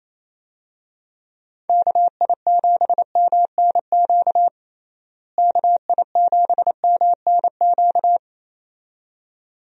Morse code, heard as a keyed tone, about 27 wpm, 700 Hz. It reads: KI7MNQ KI7MNQ
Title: USA Call Signs - 6 Characters - Repeated 2x - 28wpm